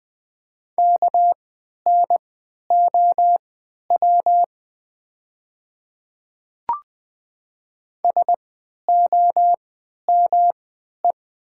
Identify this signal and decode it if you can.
Morse code, heard as a keyed tone, about 20 wpm, spaced as for 11 wpm, 700 Hz. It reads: KNOW SOME
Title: 3x Character Spacing - Top 100 Words - 20wpm